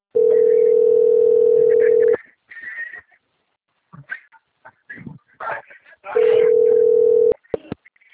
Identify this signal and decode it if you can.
Ringback tone (North America), 440 + 480 Hz.
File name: out- - - - - .56775